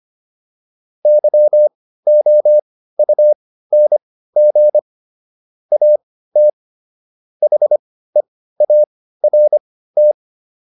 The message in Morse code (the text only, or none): YOUNG AT HEART